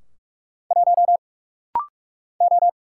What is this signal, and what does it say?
Morse code, heard as a keyed tone, about 44 wpm, 700 Hz. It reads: 1 O
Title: Single Letter-Number Alt - Rapid-Fire 45wpm